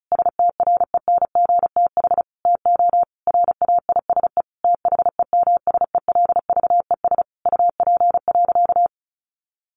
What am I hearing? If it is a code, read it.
Morse code, heard as a keyed tone, about 34 wpm, 700 Hz. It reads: STRENGTH TO RAISE THEMSELVES UP.
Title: Sentences from Top 1400 Words 35wpm